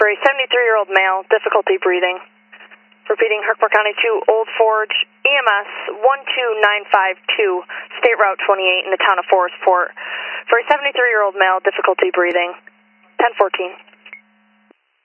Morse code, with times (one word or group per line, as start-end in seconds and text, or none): none